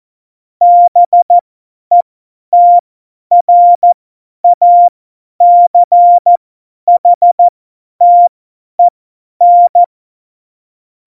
Morse code, heard as a keyed tone, about 14 wpm, spaced as for 10 wpm, 700 Hz. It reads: BETRACHTEN
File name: Morsekurs / Witze / Witz 14 April 2020, 50 F70